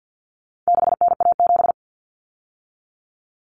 Morse code, heard as a keyed tone, about 50 wpm, 700 Hz. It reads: 6NA7
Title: Common Cut Numbers - 4-Digit 50wpm